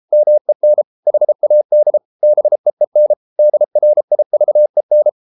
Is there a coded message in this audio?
Yes: Morse code from 0.1 to 5.1 s, heard as a keyed tone, about 33 wpm, 600 Hz. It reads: MEN HAD BEEN DRIVEN